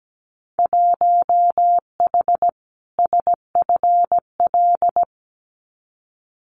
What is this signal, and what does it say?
Morse code, heard as a keyed tone, about 17 wpm, 700 Hz. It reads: 1H SFL
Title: Winter Field Day Exchanges 17wpm